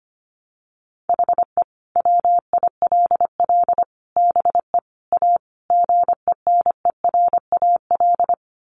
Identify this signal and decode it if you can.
Morse code, heard as a keyed tone, about 25 wpm, 700 Hz. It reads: HE WILL BE A GENERAL
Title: Sentences from Top 1200 Words 25wpm